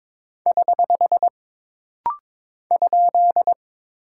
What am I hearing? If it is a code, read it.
Morse code, heard as a keyed tone, about 22 wpm, 700 Hz. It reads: <HH> ?
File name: practice / 22wpm / Single Punctuation-Prosign - Rapid-Fire - 22wpm